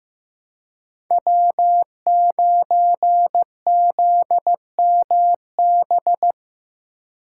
Morse code, heard as a keyed tone, about 15 wpm, 700 Hz. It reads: W9ZMB
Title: USA Call Signs - 5 Characters - 4x Character Spacing - 15wpm